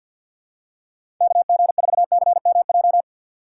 Morse code, heard as a keyed tone, about 53 wpm, 700 Hz. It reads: KG4CMJ
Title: USA Call Signs - 6 Characters - 8x Character Spacing - 50wpm